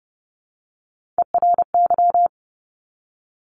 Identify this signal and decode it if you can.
Morse code, heard as a keyed tone, about 30 wpm, 690 Hz. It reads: ERY
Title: Top Three Letter Combinations in Words 30wpm